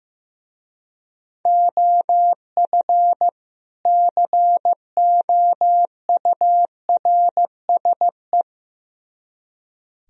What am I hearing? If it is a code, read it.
Morse code, heard as a keyed tone, about 15 wpm, 700 Hz. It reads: OF COURSE
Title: Speed-Racing - Sets of 2 Words 15wpm